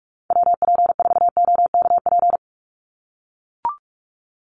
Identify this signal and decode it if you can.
Morse code, heard as a keyed tone, about 44 wpm, 700 Hz. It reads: WP4OKP
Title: USA Call Signs 45wpm